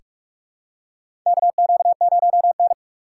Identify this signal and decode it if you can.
Morse code, heard as a keyed tone, about 44 wpm, 700 Hz. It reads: KQ0N
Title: USA Call Signs - 6x Character Spacing - 45wpm